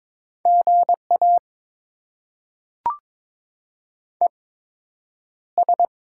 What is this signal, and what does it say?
Morse code, heard as a keyed tone, about 22 wpm, spaced as for 9 wpm, 700 Hz. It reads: GA ES
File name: Top 20 QSO Elements - 8x Character Spacing - 22wpm